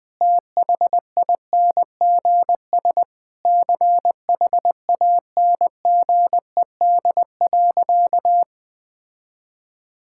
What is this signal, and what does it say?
Morse code, heard as a keyed tone, about 20 wpm, 700 Hz. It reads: THINGS CHANGED.
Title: Sentences with 5-Word Phrases - 20wpm